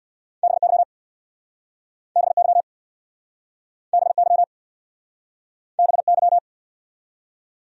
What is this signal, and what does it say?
Morse code, heard as a keyed tone, about 63 wpm, 700 Hz. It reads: BY BY BY BY